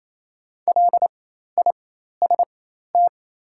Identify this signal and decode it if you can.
Morse code, heard as a keyed tone, about 28 wpm, spaced as for 13 wpm, 700 Hz. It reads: LIST
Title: 4x Character Spacing - Top 500 Words - 28wpm